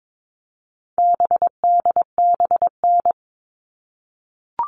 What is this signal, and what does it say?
Morse code, heard as a keyed tone, about 22 wpm, 700 Hz. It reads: BDBN
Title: All Cut Numbers - 4-Digit 22wpm